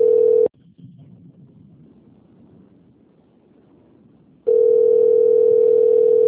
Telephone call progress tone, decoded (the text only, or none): ringback tone